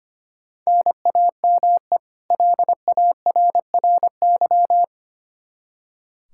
Morse code, heard as a keyed tone, about 25 wpm, 700 Hz. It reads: NAME LARRY
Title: Practice QSO - Exchange 1 - Beginning - 25wpm